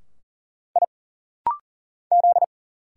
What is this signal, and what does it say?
Morse code, heard as a keyed tone, about 39 wpm, 700 Hz. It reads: I Z
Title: Single Letter-Number - Rapid-Fire 40wpm